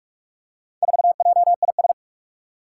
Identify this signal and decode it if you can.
Morse code, heard as a keyed tone, about 46 wpm, 700 Hz. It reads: 4JIS
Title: Four Challenging Letter-Number Combinations 45wpm